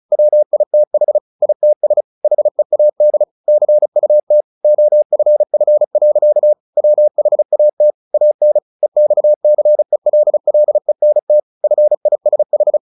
Morse code, heard as a keyed tone, about 34 wpm, 600 Hz. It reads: WITH ITS HEAD CUT OFF. WHAT AN EXCELLENT FISH
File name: the old man and the sea 2 35wpm